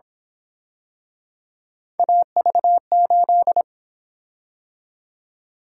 Morse code, heard as a keyed tone, about 26 wpm, 700 Hz. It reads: AV8
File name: Speed-Racing - Three Letter-Number Combinations - 22wpm